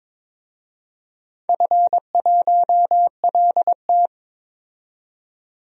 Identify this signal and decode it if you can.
Morse code, heard as a keyed tone, about 22 wpm, 700 Hz. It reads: F1LT
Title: DXCC-100 Call Signs - 4 Characters - 22wpm